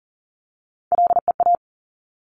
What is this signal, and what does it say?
Morse code, heard as a keyed tone, about 38 wpm, 690 Hz. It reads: LEA
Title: Top Three Letter Combinations in Words 40wpm